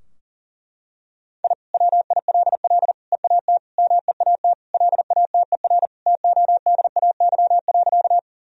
Morse code, heard as a keyed tone, about 40 wpm, 700 Hz. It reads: I WILL EAT MEAT LATER TODAY.
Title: Sentences from Top 1000 Words - Review Part 3 - 40wpm